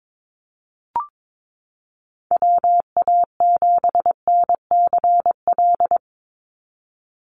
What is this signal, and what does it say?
Morse code, heard as a keyed tone, about 22 wpm, 700 Hz. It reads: WA7NCL
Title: NA SOTA Activator Callsigns 22wpm